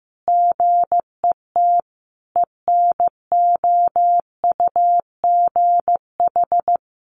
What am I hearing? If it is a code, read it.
Morse code, heard as a keyed tone, about 15 wpm, 700 Hz. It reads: GET ENOUGH
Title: Sentences from Top 1400 Words 15wpm